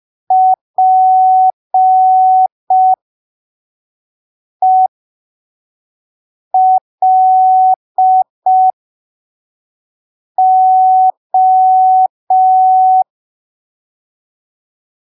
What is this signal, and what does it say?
Morse code, heard as a keyed tone, about 5 wpm, spaced as for 3 wpm, 750 Hz. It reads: PELO
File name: arquivos / texto / texto 05 5ppm